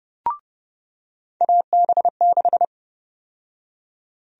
Morse code, heard as a keyed tone, about 31 wpm, 700 Hz. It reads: AB6
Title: USA Call Sign Prefixes - 30wpm